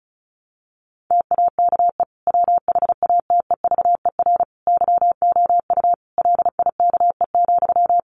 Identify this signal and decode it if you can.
Morse code, heard as a keyed tone, about 34 wpm, 700 Hz. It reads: TAKE WHATEVER YOU LIKE,
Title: Sentences from Top 1100 Words 35wpm